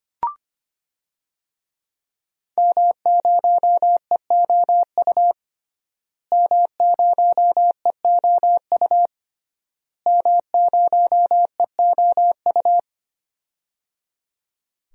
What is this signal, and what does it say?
Morse code, heard as a keyed tone, about 25 wpm, 700 Hz. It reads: M0EOU M0EOU M0EOU
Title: DXCC-100 Call Signs - 5 Characters - Repeated 3x - 25wpm